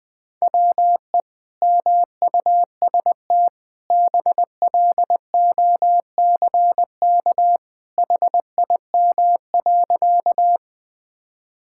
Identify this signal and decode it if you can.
Morse code, heard as a keyed tone, about 21 wpm, 700 Hz. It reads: WE MUST BLOCK HIM.
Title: Sentences from Top 1600 Words 20wpm